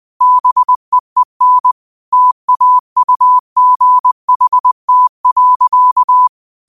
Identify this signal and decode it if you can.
Morse code, heard as a keyed tone, about 20 wpm, 1000 Hz. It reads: BEEN TAUGHT.
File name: qotdfiles / 20wpm